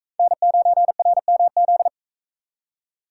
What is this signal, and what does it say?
Morse code, heard as a keyed tone, about 41 wpm, 700 Hz. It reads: N9RMZ